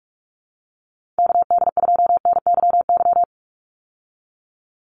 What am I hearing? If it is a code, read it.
Morse code, heard as a keyed tone, about 46 wpm, 690 Hz. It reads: KD2NYY
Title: USA Call Signs 45wpm